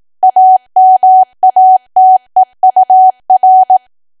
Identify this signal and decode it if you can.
Morse code, heard as a keyed tone, about 18 wpm, 750 Hz. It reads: AMATEUR
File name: Archive / 18 WPM / 18WPM